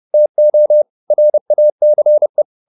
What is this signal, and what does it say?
Morse code, heard as a keyed tone, about 30 wpm, 600 Hz. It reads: TO RACE
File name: the old man and the sea / the old man and the sea 1 30wpm